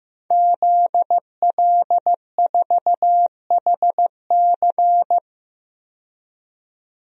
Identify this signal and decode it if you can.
Morse code, heard as a keyed tone, about 15 wpm, 700 Hz. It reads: ZL4HC
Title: DXCC-100 Call Signs - 5 Characters - 15wpm